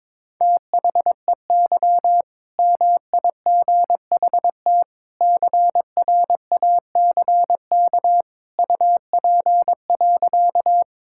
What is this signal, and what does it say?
Morse code, heard as a keyed tone, about 22 wpm, 690 Hz. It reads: THEY MIGHT CRACK UP.
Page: morse-code-ninja-practice-22wpm